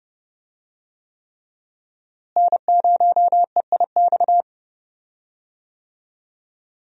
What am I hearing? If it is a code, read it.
Morse code, heard as a keyed tone, about 30 wpm, 700 Hz. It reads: N0EIX
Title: USA Call Signs - 5 Characters - Repeated 3x - 30wpm